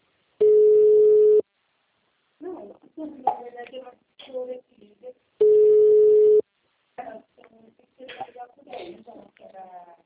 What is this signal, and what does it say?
Ringback tone (Europe, 425 Hz), with speech.